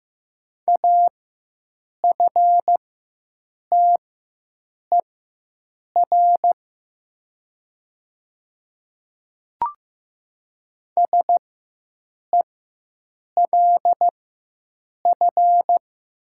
Morse code, heard as a keyed tone, about 15 wpm, spaced as for 7 wpm, 700 Hz. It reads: AFTER SELF
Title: 4x Character Spacing - Top 500 Words - 15wpm